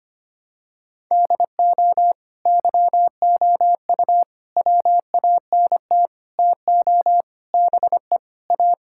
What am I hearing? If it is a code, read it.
Morse code, heard as a keyed tone, about 25 wpm, 700 Hz. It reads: DO YOU WANT TO BE A
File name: practice / 25wpm / Sentences with 4-Word Phrases - 25wpm